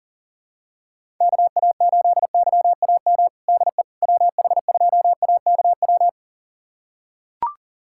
Morse code, heard as a keyed tone, about 40 wpm, 700 Hz. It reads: KA8YAM DE WH2AKW
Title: USA Callsign DE Callsign - 6 Characters - 40wpm